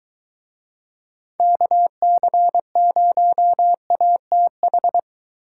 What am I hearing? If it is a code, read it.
Morse code, heard as a keyed tone, about 24 wpm, 700 Hz. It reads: KC0ATH